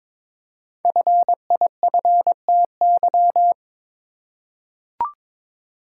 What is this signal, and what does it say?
Morse code, heard as a keyed tone, about 22 wpm, 700 Hz. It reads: FIFTY
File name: 5 Letter Words - First 3 Letters Given - 22wpm